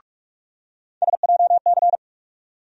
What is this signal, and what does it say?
Morse code, heard as a keyed tone, about 46 wpm, 700 Hz. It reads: SJC
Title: USA Call Sign Postfixes - 3 Letters 45wpm